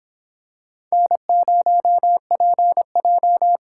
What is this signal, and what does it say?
Morse code, heard as a keyed tone, about 26 wpm, 700 Hz. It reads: N0PJ